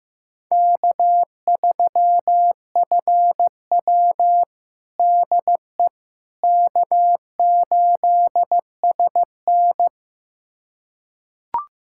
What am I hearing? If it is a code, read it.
Morse code, heard as a keyed tone, about 15 wpm, 700 Hz. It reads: K3FW DE K8SN
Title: USA Callsign DE Callsign - 4 Characters - 15wpm